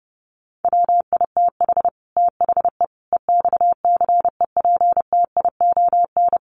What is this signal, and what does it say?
Morse code, heard as a keyed tone, about 30 wpm, 700 Hz. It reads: WITH THE EXCEPTION
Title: Sets of 5 Words Encore 30wpm